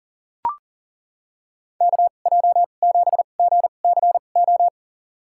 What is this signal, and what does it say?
Morse code, heard as a keyed tone, about 40 wpm, spaced as for 29 wpm, 700 Hz. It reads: KJ7GCO
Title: USA Call Signs - 2x Character Spacing - 40wpm